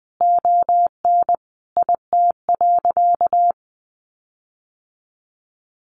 Morse code, heard as a keyed tone, about 20 wpm, 700 Hz. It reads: ON IT.
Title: Sentences from Top 1300 Words 20wpm